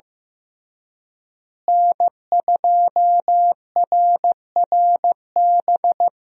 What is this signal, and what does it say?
Morse code, heard as a keyed tone, about 15 wpm, 700 Hz. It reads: N2RRB
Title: USA Call Signs - 5 Characters - Repeated 3x - 15wpm